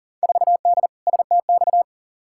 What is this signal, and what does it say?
Morse code, heard as a keyed tone, about 37 wpm, 700 Hz. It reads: 4D STX
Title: Field Day Exchanges Encore - 40wpm